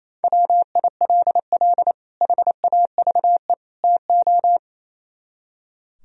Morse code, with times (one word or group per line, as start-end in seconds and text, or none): 0.2-1.9 WILL
2.2-3.5 HAVE
3.8-4.6 TO